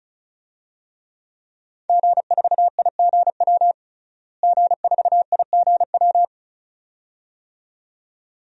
Morse code, heard as a keyed tone, about 34 wpm, 700 Hz. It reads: G4IGW G4IGW
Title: DXCC-100 Call Signs - 5 Characters - Repeated 2x - 35wpm